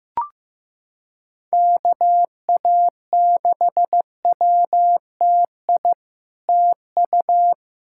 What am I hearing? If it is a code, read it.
Morse code, heard as a keyed tone, about 15 wpm, 700 Hz. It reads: KA6WTI TU
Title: SOTA QSO Practice 15wpm